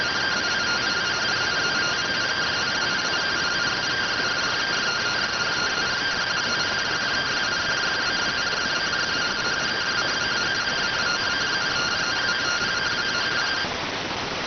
Two-tone teleprinter signal, about 170 Hz shift.